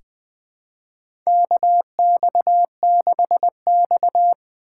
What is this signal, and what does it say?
Morse code, heard as a keyed tone, about 20 wpm, 700 Hz. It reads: KX6X
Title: USA Callsign DE Callsign - 4 Characters - 20wpm